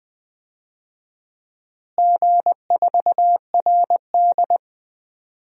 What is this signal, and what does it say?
Morse code, heard as a keyed tone, about 20 wpm, 700 Hz. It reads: G4RD
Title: DXCC-100 Call Signs - 4 Characters - Repeated 3x - 20wpm